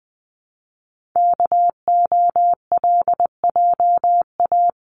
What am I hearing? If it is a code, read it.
Morse code, heard as a keyed tone, about 20 wpm, 700 Hz. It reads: KOLJA
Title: CWT Exchange 20wpm